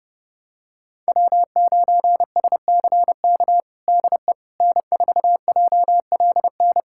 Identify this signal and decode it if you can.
Morse code, heard as a keyed tone, about 30 wpm, 700 Hz. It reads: W9SCK DE N4JLN
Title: USA Callsign DE Callsign - 5 Characters - 30wpm